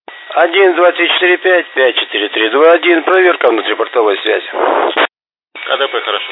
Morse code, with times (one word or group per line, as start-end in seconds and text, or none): none